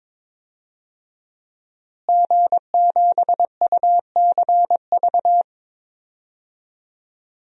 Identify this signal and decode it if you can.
Morse code, heard as a keyed tone, about 22 wpm, 700 Hz. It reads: G7UCV
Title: DXCC-100 Call Signs - 5 Characters - Repeated 2x - 22wpm